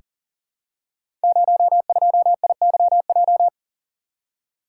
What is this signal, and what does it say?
Morse code, heard as a keyed tone, about 42 wpm, 700 Hz. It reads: 02IYJ
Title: Code Groups - 5 Letter:Numbers - 40wpm